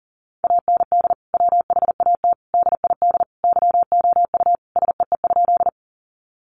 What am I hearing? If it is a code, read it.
Morse code, heard as a keyed tone, about 40 wpm, 700 Hz. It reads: AND WHAT DID YOU SEE?
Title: Sentences from Top 100 Words 40wpm